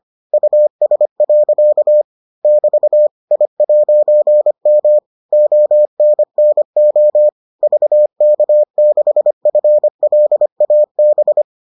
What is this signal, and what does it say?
Morse code, heard as a keyed tone, about 25 wpm, 600 Hz. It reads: US. <BT> I'M ONNO VK6FLAB